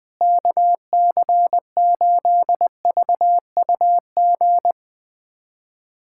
Morse code, heard as a keyed tone, about 20 wpm, 700 Hz. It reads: KC8VUG
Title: USA Callsign DE Callsign - 6 Characters - 20wpm